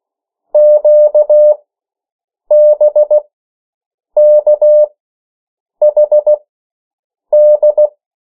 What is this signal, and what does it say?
Morse code, heard as a keyed tone, about 16 wpm, spaced as for 7 wpm, 600 Hz. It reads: QBKHD
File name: morse mp3 / 14 urok 13